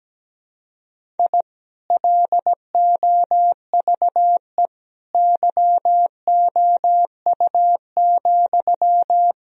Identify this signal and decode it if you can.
Morse code, heard as a keyed tone, about 17 wpm, 700 Hz. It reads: I LOVE YOU,
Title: Sentences from Top 1000 Words - Review Part 2 - 17wpm